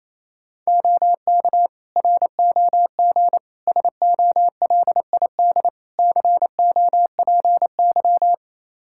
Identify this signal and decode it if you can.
Morse code, heard as a keyed tone, about 29 wpm, 700 Hz. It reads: OK ROG SOLID COPY
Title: Practice QSO - Exchange 3 - More Info - 28wpm